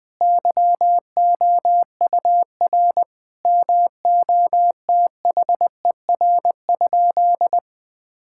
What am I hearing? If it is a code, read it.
Morse code, heard as a keyed tone, about 21 wpm, 700 Hz. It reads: YOUR MOTHER?
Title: Sentences from Top 1600 Words 20wpm